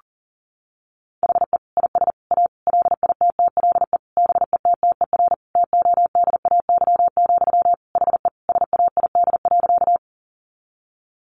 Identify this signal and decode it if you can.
Morse code, heard as a keyed tone, about 40 wpm, 700 Hz. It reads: HE IS A LITTLE BETTER TODAY, HE SAID.
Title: Sentences from Top 600 Words 40wpm